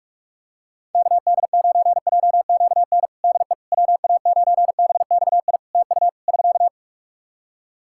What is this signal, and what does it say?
Morse code, heard as a keyed tone, about 47 wpm, 700 Hz. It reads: KD9JQN DE WA9BXI TU <SK>
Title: Practice QSO - Exchange 4 - Ending - 45wpm